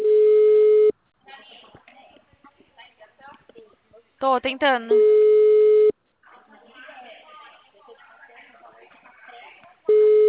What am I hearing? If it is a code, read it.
Ringback tone (Europe, 425 Hz), with speech.